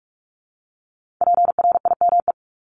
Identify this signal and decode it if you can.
Morse code, heard as a keyed tone, about 46 wpm, 700 Hz. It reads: PRIME
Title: Multi-letter Phonemes 45wpm